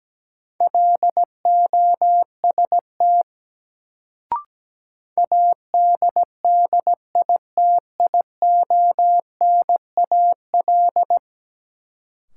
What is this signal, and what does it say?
Morse code, heard as a keyed tone, about 17 wpm, 700 Hz. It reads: LOST ADDITIONAL
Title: Sentences from Top 1400 Words - New Word Drill Alt - 17wpm